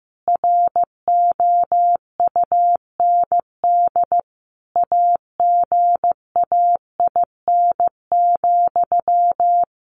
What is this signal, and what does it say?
Morse code, heard as a keyed tone, about 15 wpm, 700 Hz. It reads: ROUND AGAIN,
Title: Sentences from Top 600 Words 15wpm